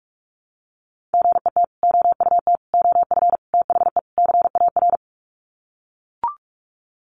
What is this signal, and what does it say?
Morse code, heard as a keyed tone, about 44 wpm, 700 Hz. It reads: GET OUT OF THE CAR
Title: Sets of 5 Words Encore2 45wpm